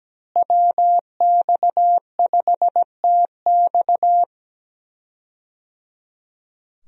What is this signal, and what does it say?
Morse code, heard as a keyed tone, about 17 wpm, 700 Hz. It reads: WX5TX
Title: USA Call Signs - 5 Characters - Repeated 3x - 17wpm